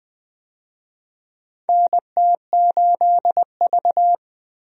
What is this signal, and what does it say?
Morse code, heard as a keyed tone, about 20 wpm, 700 Hz. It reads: NT8V
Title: USA Call Signs - 4 Characters - 20wpm